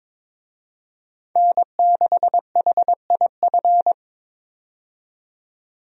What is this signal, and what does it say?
Morse code, heard as a keyed tone, about 22 wpm, 700 Hz. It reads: N6HIF
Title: USA Call Signs - 5 Characters - 22wpm